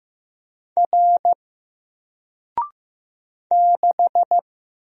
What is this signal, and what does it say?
Morse code, heard as a keyed tone, about 15 wpm, 700 Hz. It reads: R 6
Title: Single Character (Letter:Number:Punctuation:Prosign) 15wpm